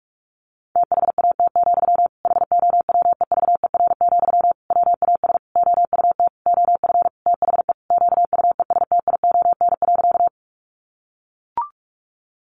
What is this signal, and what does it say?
Morse code, heard as a keyed tone, about 44 wpm, 700 Hz. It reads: THAT, HOWEVER, WAS OUT OF THE QUESTION.
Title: Sentences from Top 600 Words 45wpm